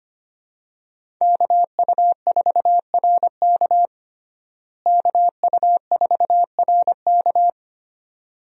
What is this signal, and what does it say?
Morse code, heard as a keyed tone, about 25 wpm, 700 Hz. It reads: KU4RK KU4RK
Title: USA Call Signs - 5 Characters - Repeated 2x - 25wpm